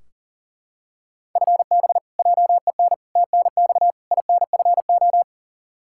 Morse code, heard as a keyed tone, about 40 wpm, 700 Hz. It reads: FB JEN TNX INFO
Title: Practice QSO - Exchange 3 - More Info - 40wpm